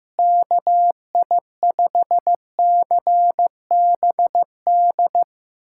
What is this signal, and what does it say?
Morse code, heard as a keyed tone, about 15 wpm, 700 Hz. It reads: KI5CBD
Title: USA Callsign DE Callsign - 6 Characters - 15wpm